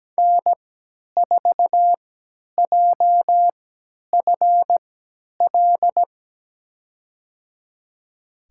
Morse code, heard as a keyed tone, about 17 wpm, spaced as for 10 wpm, 700 Hz. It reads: N4JFL
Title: USA Call Signs - 5 Characters - 3x Character Spacing - 17wpm